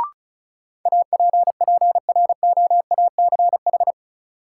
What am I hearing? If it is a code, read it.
Morse code, heard as a keyed tone, about 36 wpm, 700 Hz. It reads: APPROACH
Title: Sentences from Top 1000 Words - Review Part 6 - New Word Drill - 35wpm